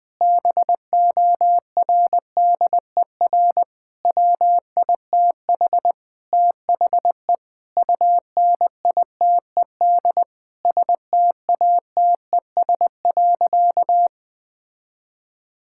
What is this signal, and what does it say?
Morse code, heard as a keyed tone, about 21 wpm, 700 Hz. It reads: BORDER WITH THE UNITED STATES.
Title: Sentences with 4-Word Phrases - Spoken Beginning - 20wpm